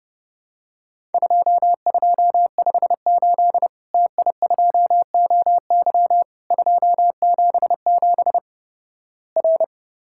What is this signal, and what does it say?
Morse code, heard as a keyed tone, about 30 wpm, 700 Hz. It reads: 2258 TI2OY 277 R
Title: Worked All Europe DX Contest - QTC Exchange - 30wpm